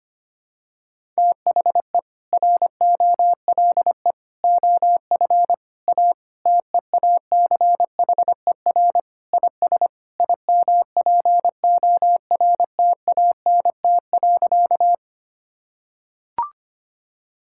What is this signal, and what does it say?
Morse code, heard as a keyed tone, about 25 wpm, 700 Hz. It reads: THE ROLE OF A TEACHER IS IMPORTANT.
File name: practice / 25wpm / Sentences with 3-Word Phrases - 25wpm